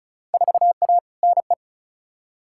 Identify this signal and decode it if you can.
Morse code, heard as a keyed tone, about 36 wpm, 700 Hz. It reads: 4A NE
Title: Field Day Exchanges - 35wpm